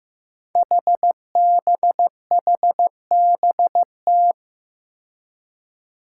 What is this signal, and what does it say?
Morse code, heard as a keyed tone, about 15 wpm, 700 Hz. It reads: HBHBT
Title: Five Challenging Letter-Number Combinations 15wpm